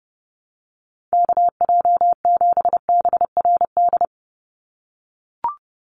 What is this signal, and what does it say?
Morse code, heard as a keyed tone, about 30 wpm, 700 Hz. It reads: KJ7BRD
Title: USA Call Signs 30wpm